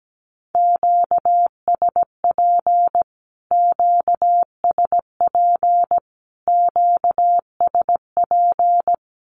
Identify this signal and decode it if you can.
Morse code, heard as a keyed tone, about 17 wpm, 700 Hz. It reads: QSP QSP QSP
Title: Q-Codes Repeated 3x 17wpm